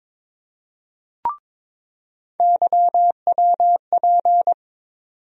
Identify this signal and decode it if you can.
Morse code, heard as a keyed tone, about 22 wpm, 700 Hz. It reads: YWP
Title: USA Call Sign Postfixes - 3 Letters 22wpm